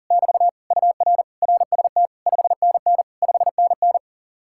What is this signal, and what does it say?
Morse code, heard as a keyed tone, about 49 wpm, 700 Hz. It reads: <BT> UR RST 5NN 5NN
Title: Speed-Racing Common QSO Phrases Encore - 40wpm